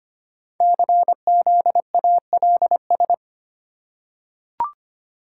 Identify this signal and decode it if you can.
Morse code, heard as a keyed tone, about 25 wpm, 700 Hz. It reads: CZALS
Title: Code Groups - 5 Letters - 25wpm